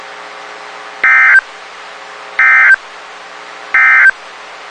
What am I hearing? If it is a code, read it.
Emergency Alert System (SAME) tones, heard as harsh buzzing bursts. It is heard from 0.9 to 4.2 s.
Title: WXK91 NOAA Weather Radio